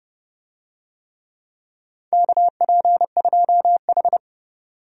Morse code, heard as a keyed tone, about 31 wpm, 700 Hz. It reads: KP2H
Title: DXCC-100 Call Signs - 4 Characters - 30wpm